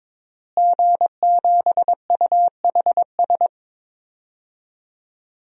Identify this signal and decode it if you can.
Morse code, heard as a keyed tone, about 22 wpm, 700 Hz. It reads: G7UHS
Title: DXCC-100 Call Signs - 5 Characters - Repeated 3x - 22wpm